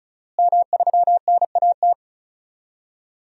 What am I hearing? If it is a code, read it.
Morse code, heard as a keyed tone, about 34 wpm, 700 Hz. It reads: M3NAT
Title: DXCC-100 Call Signs - 5 Characters - Repeated 2x - 35wpm